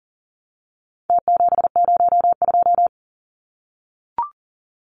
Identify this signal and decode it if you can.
Morse code, heard as a keyed tone, about 40 wpm, 700 Hz. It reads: T702
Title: Numbers - Contest Style - 4-Digit 40wpm